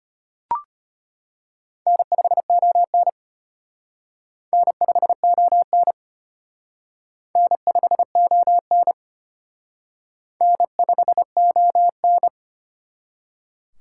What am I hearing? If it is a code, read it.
Morse code, heard as a keyed tone, about 31 wpm, 700 Hz. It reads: N5ON N5ON N5ON N5ON
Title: Speed Racing - USA Call Signs 25wpm